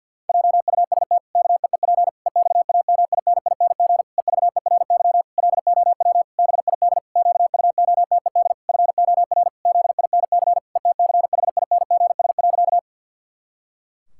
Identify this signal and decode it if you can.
Morse code, heard as a keyed tone, about 50 wpm, 700 Hz. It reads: JUST KEEP EXAMINING EVERY LOW BID QUOTED FOR ZINC ETCHINGS.
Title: Pangram Challenge v2 - 50wpm